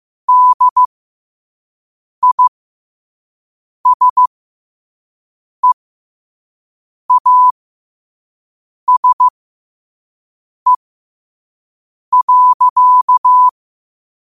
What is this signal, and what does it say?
Morse code, heard as a keyed tone, about 15 wpm, spaced as for 5 wpm, 1000 Hz. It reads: DISEASE.
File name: qotdfiles / 5wpm